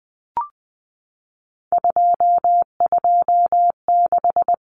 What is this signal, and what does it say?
Morse code, heard as a keyed tone, about 20 wpm, 700 Hz. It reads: 226
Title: Common Cut Numbers - 3-Digit 20wpm